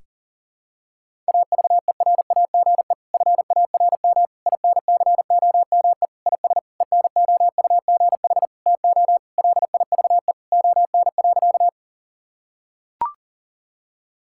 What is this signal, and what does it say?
Morse code, heard as a keyed tone, about 37 wpm, 700 Hz. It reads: AVERAGE FARM INCOME IS ENOUGH TO LIVE ON.
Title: Sentences from Top 1000 Words - Review Part 6 - 40wpm